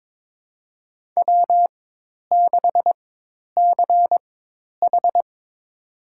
Morse code, heard as a keyed tone, about 21 wpm, spaced as for 10 wpm, 700 Hz. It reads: W6CH
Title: USA Call Signs - 4x Character Spacing - 22wpm